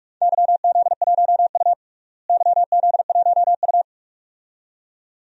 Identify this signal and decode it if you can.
Morse code, heard as a keyed tone, about 47 wpm, 700 Hz. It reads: YZ1U YZ1U
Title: DXCC-100 Call Signs - 4 Characters - Repeated 2x - 45wpm